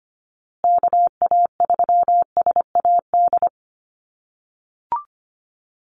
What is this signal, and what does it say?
Morse code, heard as a keyed tone, about 25 wpm, 700 Hz. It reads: KA3SAD
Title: USA Call Signs Repeated 2x 25wpm